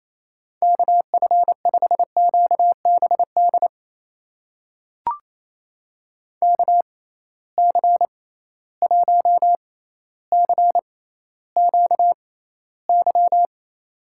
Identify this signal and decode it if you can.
Morse code, heard as a keyed tone, about 27 wpm, spaced as for 14 wpm, 700 Hz. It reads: KF5QBD KC1CQY
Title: USA Call Signs - 6x Character Spacing - 28wpm